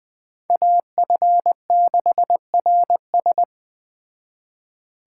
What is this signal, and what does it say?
Morse code, heard as a keyed tone, about 20 wpm, 700 Hz. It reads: AF6RS
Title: Speed Racing - USA 2x2 Call Signs 17wpm